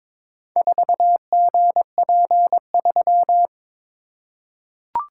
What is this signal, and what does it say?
Morse code, heard as a keyed tone, about 22 wpm, 700 Hz. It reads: 4GP3
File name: practice / 22wpm / Four Letter-Number Combinations - 22wpm